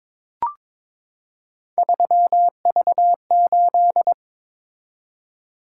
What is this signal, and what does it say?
Morse code, heard as a keyed tone, about 22 wpm, 700 Hz. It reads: 3V8
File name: practice / 22wpm / Three Letter-Number Combinations - 22wpm